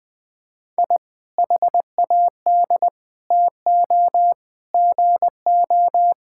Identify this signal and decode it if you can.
Morse code, heard as a keyed tone, about 21 wpm, 700 Hz. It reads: I HAD TO GO
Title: Sentences with 3-Word Phrases - 20wpm